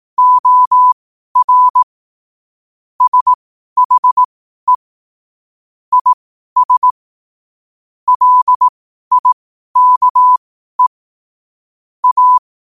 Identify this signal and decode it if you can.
Morse code, heard as a keyed tone, about 18 wpm, spaced as for 12 wpm, 1000 Hz. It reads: OR SHE IS LIKE A
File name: qotdfiles / 12wpm